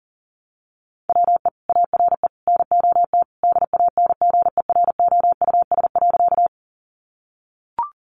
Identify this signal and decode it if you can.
Morse code, heard as a keyed tone, about 37 wpm, 700 Hz. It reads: WE ARE NOT DANGEROUS.
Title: Sentences from Top 1200 Words 40wpm